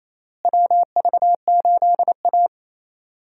Morse code, heard as a keyed tone, about 28 wpm, 700 Hz. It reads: WV8A
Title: USA Call Signs - 4 Characters - Repeated 3x - 28wpm